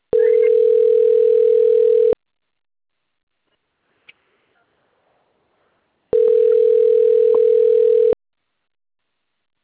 Ringback tone (North America), 440 + 480 Hz.
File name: rg-1600-unknown- - - .39715